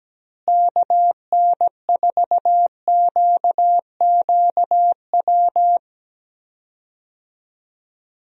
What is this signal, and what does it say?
Morse code, heard as a keyed tone, about 17 wpm, 700 Hz. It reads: KN4QQW
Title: USA Call Signs - 6 Characters - Repeated 2x - 17wpm